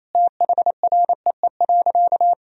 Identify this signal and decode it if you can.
Morse code, heard as a keyed tone, about 28 wpm, 700 Hz. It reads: THREE.
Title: Sentences from Top 1000 Words - Review Part 1 - 28wpm